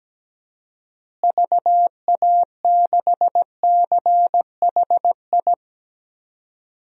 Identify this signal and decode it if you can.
Morse code, heard as a keyed tone, about 17 wpm, 700 Hz. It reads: VA6CHI
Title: DXCC-100 Call Signs - 6 Characters - Repeated 2x - 17wpm